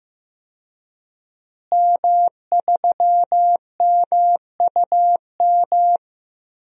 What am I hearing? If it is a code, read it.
Morse code, heard as a keyed tone, about 15 wpm, 700 Hz. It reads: M3MUM
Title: DXCC-100 Call Signs - 5 Characters - Repeated 2x - 15wpm